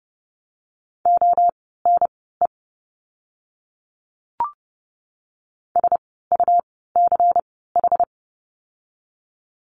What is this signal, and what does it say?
Morse code, heard as a keyed tone, about 30 wpm, spaced as for 17 wpm, 700 Hz. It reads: ONE SUCH
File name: course / 30:10wpm / 237 Lesson 38f - Letter q - Words 30wpm